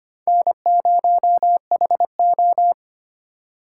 Morse code, heard as a keyed tone, about 32 wpm, 700 Hz. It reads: N0HO